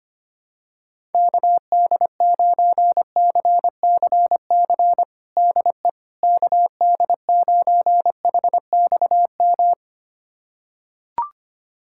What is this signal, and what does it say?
Morse code, heard as a keyed tone, about 25 wpm, 700 Hz. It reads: KD9CCC DE KD9HXM
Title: USA Callsign DE Callsign - 6 Characters - 25wpm